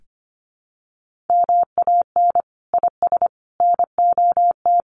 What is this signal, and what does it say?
Morse code, heard as a keyed tone, about 25 wpm, 700 Hz. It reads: MAN IS NOT
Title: Sentences from Top 1400 Words 25wpm